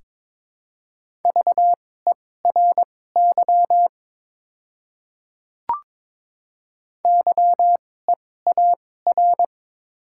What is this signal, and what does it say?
Morse code, heard as a keyed tone, about 22 wpm, spaced as for 16 wpm, 700 Hz. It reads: VERY YEAR
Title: 2x Character Spacing - Top 300 Words - 22wpm